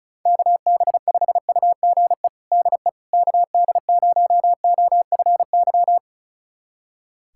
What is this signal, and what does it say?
Morse code, heard as a keyed tone, about 36 wpm, 700 Hz. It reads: KB5UGE DE KD0OFY
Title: USA Callsign DE Callsign 35wpm